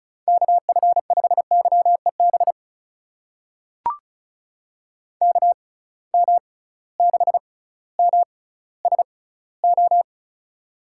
Morse code, heard as a keyed tone, about 36 wpm, spaced as for 18 wpm, 700 Hz. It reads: KF5YEB KM6MSO
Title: USA Call Signs - 6 Characters - 6x Character Spacing - 35wpm